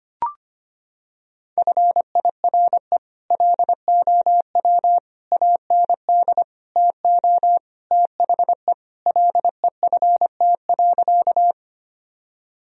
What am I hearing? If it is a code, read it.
Morse code, heard as a keyed tone, about 25 wpm, 700 Hz. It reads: FIRE LOW AND TO THE LEFT.
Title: Sentences from Top 1000 Words - Review Part 1 - 25wpm